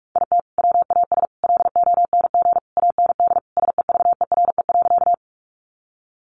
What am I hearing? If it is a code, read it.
Morse code, heard as a keyed tone, about 44 wpm, 700 Hz. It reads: IT WAS LONG AND SEVERE.
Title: Sentences from Top 1400 Words 45wpm